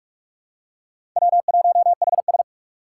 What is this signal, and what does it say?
Morse code, heard as a keyed tone, about 46 wpm, 700 Hz. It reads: W1HS
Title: USA Call Signs - 4 Characters - 2x Character Spacing - 45wpm